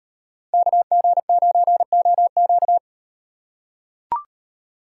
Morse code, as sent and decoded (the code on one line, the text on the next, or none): -.- --. ----. --- --.-
KG9OQ